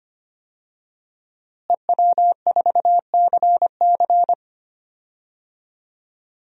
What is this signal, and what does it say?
Morse code, heard as a keyed tone, about 25 wpm, 700 Hz. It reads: EW4CC
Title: DXCC-100 Call Signs - 5 Characters - Repeated 3x - 25wpm